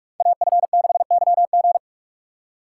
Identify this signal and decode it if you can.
Morse code, heard as a keyed tone, about 46 wpm, 700 Hz. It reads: AF6YG
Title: USA Call Signs - 5 Characters - 2x Character Spacing - 45wpm